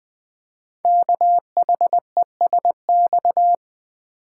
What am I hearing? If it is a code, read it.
Morse code, heard as a keyed tone, about 20 wpm, 700 Hz. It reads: KHESX
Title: Code Groups - 5 Letters - 20wpm